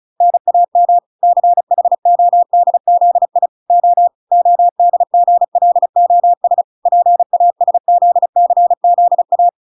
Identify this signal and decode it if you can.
Morse code, heard as a keyed tone, about 34 wpm, 700 Hz. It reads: NAM CHODZI O ODGLOS PASZCZA